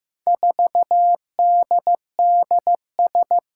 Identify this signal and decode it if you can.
Morse code, heard as a keyed tone, about 15 wpm, 700 Hz. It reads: 4DDS
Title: Four Challenging Letter-Number Combinations 15wpm